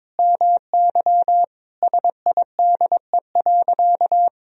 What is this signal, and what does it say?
Morse code, heard as a keyed tone, about 22 wpm, 700 Hz. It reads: MY SIDE.
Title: Sentences from Top 1000 Words - Review Part 1 - 22wpm